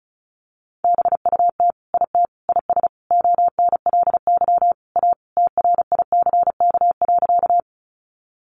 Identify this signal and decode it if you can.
Morse code, heard as a keyed tone, about 34 wpm, 700 Hz. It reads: BUT IT IS ONLY A TRICK.
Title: Sentences from Top 1500 Words 35wpm